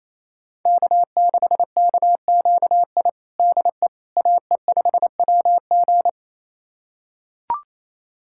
Morse code, heard as a keyed tone, about 28 wpm, 700 Hz. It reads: K6KQI DE AE5WG
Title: USA Callsign DE Callsign - 5 Characters - 28wpm